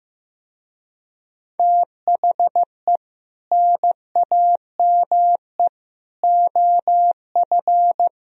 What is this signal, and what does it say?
Morse code, heard as a keyed tone, about 17 wpm, 700 Hz. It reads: THE NAME OF
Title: Speed-Racing - Sets of 3 Words 15wpm